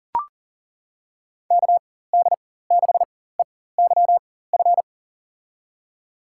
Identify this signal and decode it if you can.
Morse code, heard as a keyed tone, about 40 wpm, spaced as for 19 wpm, 700 Hz. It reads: KD6EYF
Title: USA Call Signs - 4x Character Spacing - 40wpm